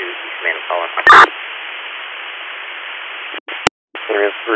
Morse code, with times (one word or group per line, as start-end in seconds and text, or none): none